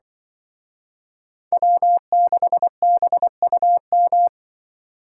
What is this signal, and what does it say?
Morse code, heard as a keyed tone, about 24 wpm, 700 Hz. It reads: W6BUM
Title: Speed Racing - USA Call Signs 20wpm